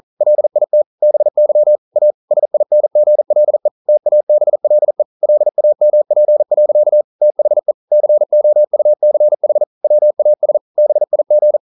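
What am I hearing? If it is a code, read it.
Morse code, heard as a keyed tone, about 43 wpm, 600 Hz. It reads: LIT BY A SINGLE TABLE LAMP. THE COUCH WAS BIG
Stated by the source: Morse code practice at 41 wpm, from CW med Gnister.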